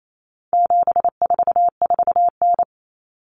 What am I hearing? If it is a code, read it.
Morse code, heard as a keyed tone, about 28 wpm, 700 Hz. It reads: 744N
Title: Common Cut Numbers - 4-Digit 28wpm